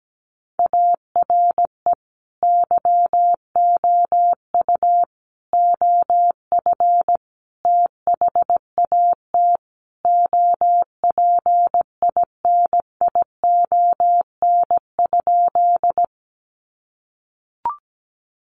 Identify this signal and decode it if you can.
Morse code, heard as a keyed tone, about 17 wpm, 700 Hz. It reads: ARE YOU OF THAT OPINION?
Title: Sentences from Top 1200 Words 17wpm